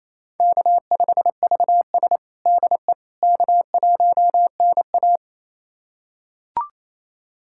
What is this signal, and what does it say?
Morse code, heard as a keyed tone, about 28 wpm, 700 Hz. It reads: K5VS DE K1NA
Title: USA Callsign DE Callsign - 4 Characters - 28wpm